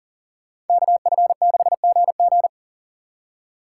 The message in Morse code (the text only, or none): KF6GG